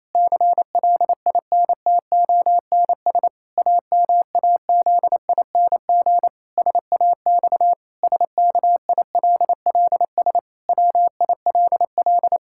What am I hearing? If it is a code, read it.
Morse code, heard as a keyed tone, about 28 wpm, 700 Hz. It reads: CLINTONS AMAZING SAX SKILLS WILL
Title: Pangram Challenge v2 - 28wpm